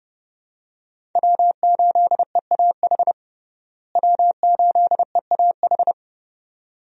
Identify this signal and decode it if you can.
Morse code, heard as a keyed tone, about 30 wpm, 700 Hz. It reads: W8EAH W8EAH
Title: USA Call Signs - 5 Characters - Repeated 2x - 30wpm